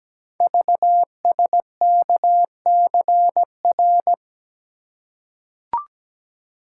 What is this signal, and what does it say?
Morse code, heard as a keyed tone, about 17 wpm, 700 Hz. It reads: VSKCR
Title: Code Groups - 5 Letters - 17wpm